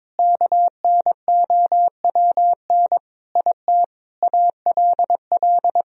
Morse code, heard as a keyed tone, about 22 wpm, 700 Hz. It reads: KNOWN IT ALL
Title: Sentences from Top 1000 Words - Review Part 3 - 22wpm